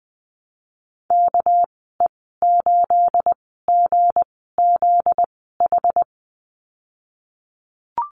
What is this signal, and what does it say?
Morse code, heard as a keyed tone, about 20 wpm, spaced as for 14 wpm, 700 Hz. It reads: KE8GZH
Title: 245 Lesson 39g - Letter z - Call Signs 20wpm